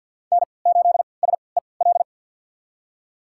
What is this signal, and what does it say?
Morse code, heard as a keyed tone, about 47 wpm, spaced as for 28 wpm, 700 Hz. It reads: N8SEL
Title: USA Call Signs - 5 Characters - 3x Character Spacing - 50wpm